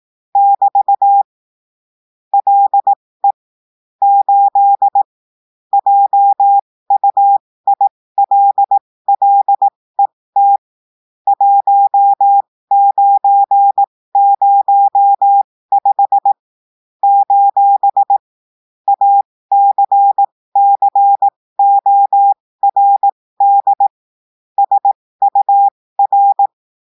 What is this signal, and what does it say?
Morse code, heard as a keyed tone, about 18 wpm, spaced as for 15 wpm, 800 Hz. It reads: <BT> LE 8 JUILLET 1905 : ACCORD SUR
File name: cours-du-lundi files / cours-1580-texte-clair-15-mpm